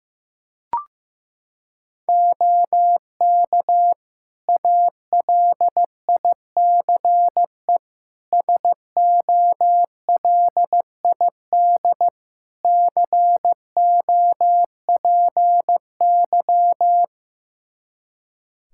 Morse code, heard as a keyed tone, about 15 wpm, 700 Hz. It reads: OK ALICE SOLID COPY
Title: Practice QSO - Exchange 3 - More Info - 15wpm